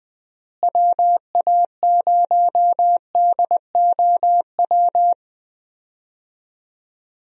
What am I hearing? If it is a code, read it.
Morse code, heard as a keyed tone, about 21 wpm, 700 Hz. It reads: WA0DOW
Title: USA Call Signs - 6 Characters - Repeated 2x - 20wpm